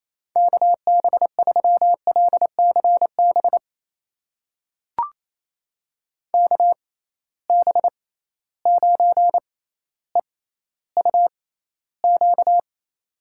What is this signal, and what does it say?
Morse code, heard as a keyed tone, about 28 wpm, spaced as for 14 wpm, 700 Hz. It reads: KB3LCB KB9EUQ
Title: USA Call Signs - 6 Characters - 6x Character Spacing - 28wpm